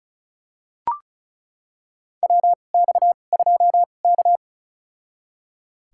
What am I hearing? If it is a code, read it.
Morse code, heard as a keyed tone, about 36 wpm, spaced as for 25 wpm, 700 Hz. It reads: WX2K
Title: USA Call Signs - 4 Characters - 2x Character Spacing - 35wpm